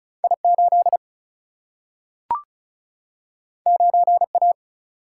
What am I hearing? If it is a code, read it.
Morse code, heard as a keyed tone, about 36 wpm, 700 Hz. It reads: I8 9A
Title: Single-Letter Single-Number 35wpm